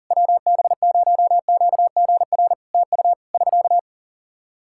Morse code, heard as a keyed tone, about 40 wpm, 700 Hz. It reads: WB0QGR TU <SK>
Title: Practice QSO - Exchange 4 - Ending - 40wpm